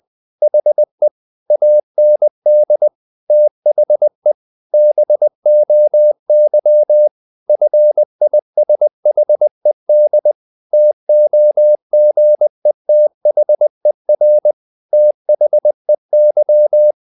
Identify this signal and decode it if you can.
Morse code, heard as a keyed tone, about 20 wpm, 600 Hz. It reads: HE AND THE BOY FISHED TOGETHER THEY
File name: the old man and the sea / the old man and the sea part 1 20wpm